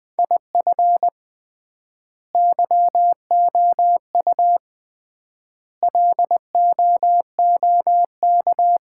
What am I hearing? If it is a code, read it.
Morse code, heard as a keyed tone, about 20 wpm, 700 Hz. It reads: IF YOU LOOK
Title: 3x Word-Spacing - Sets of 5 Words - 20wpm